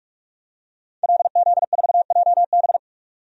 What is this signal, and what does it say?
Morse code, heard as a keyed tone, about 43 wpm, 700 Hz. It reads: LZ4JB
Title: DXCC-100 Call Signs - 5 Characters - 45wpm